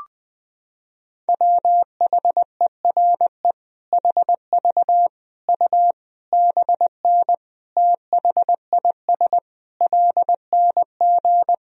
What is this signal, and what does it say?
Morse code, heard as a keyed tone, about 21 wpm, 700 Hz. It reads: WHERE HV U BN THIS LNG
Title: Sentences from Top 300 Words and Abbreviations 20wpm